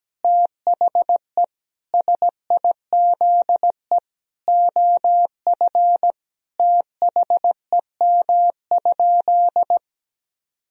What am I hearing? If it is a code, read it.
Morse code, heard as a keyed tone, about 17 wpm, 700 Hz. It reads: THE SIZE OF THEM?
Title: Sentences from Top 1000 Words - Review Part 2 - 17wpm